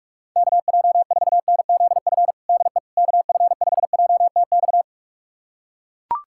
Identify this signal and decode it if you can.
Morse code, heard as a keyed tone, about 41 wpm, 700 Hz. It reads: KJ4NZF DE KF5JTX
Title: USA Callsign DE Callsign - 6 Characters - 45wpm